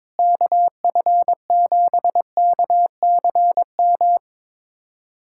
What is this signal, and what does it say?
Morse code, heard as a keyed tone, about 22 wpm, 700 Hz. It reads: KF7KCM
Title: USA Callsign DE Callsign - 6 Characters - 22wpm